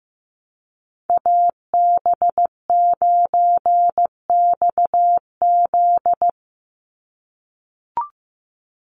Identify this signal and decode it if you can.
Morse code, heard as a keyed tone, about 15 wpm, 700 Hz. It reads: AB9XZ
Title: USA Call Signs 15wpm